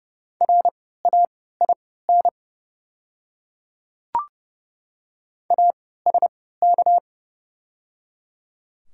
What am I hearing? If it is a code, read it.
Morse code, heard as a keyed tone, about 30 wpm, spaced as for 17 wpm, 690 Hz. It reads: RAIN ASK
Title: 3x Character Spacing - Top 300 Words - 30wpm